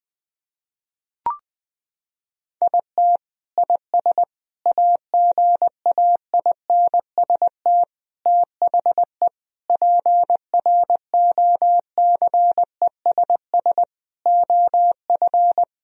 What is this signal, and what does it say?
Morse code, heard as a keyed tone, about 20 wpm, 700 Hz. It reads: IT IS AGAINST THE PROCESS OF
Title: Sentences from Top 1000 Words - Review Part 3 - 20wpm